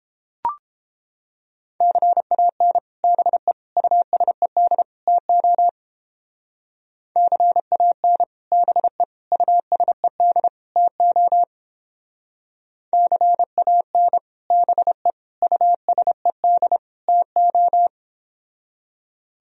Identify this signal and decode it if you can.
Morse code, heard as a keyed tone, about 29 wpm, 700 Hz. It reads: CAN BE USED TO CAN BE USED TO CAN BE USED TO